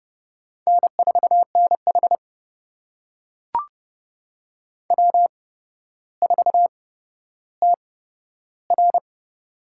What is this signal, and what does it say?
Morse code, heard as a keyed tone, about 30 wpm, spaced as for 13 wpm, 700 Hz. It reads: N4NH W4TR
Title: USA Call Signs - 4 Characters - 8x Character Spacing - 30wpm